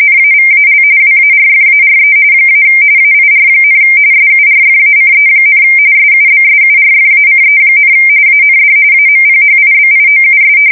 RTTY, about 45.45 baud. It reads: NAVAL WARFARE - AIR, SURFACE, SUBSURFACE, AND AMPHIBIOUS
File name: fsk170 / battleshipnj